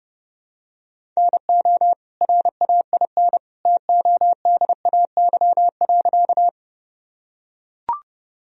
Morse code, heard as a keyed tone, about 30 wpm, 700 Hz. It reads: NO RAIN TODAY.
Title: Sentences from Top 1000 Words - Review Part 1 - 30wpm